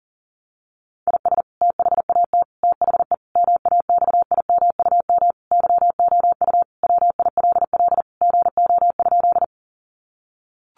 Morse code, heard as a keyed tone, about 40 wpm, 700 Hz. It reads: IS THAT THE MAXIMUM YOU WILL GO?
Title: Sentences from Top 1300 Words 40wpm